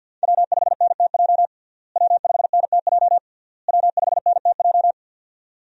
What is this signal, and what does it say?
Morse code, heard as a keyed tone, about 47 wpm, 690 Hz. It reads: W5NTJ W5NTJ W5NTJ
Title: USA Call Signs - 5 Characters - Repeated 3x - 50wpm